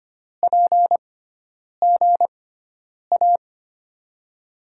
Morse code, heard as a keyed tone, about 25 wpm, spaced as for 9 wpm, 700 Hz. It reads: PGA
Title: USA Call Signs - 5 Characters - 6x Character Spacing - 25wpm